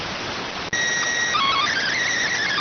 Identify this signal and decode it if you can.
SSTV (slow-scan television), PD 290.